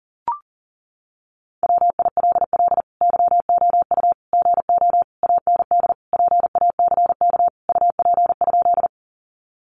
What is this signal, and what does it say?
Morse code, heard as a keyed tone, about 40 wpm, 700 Hz. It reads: WILL YOU GO AND PACK UP?
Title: Sentences from Top 1300 Words 40wpm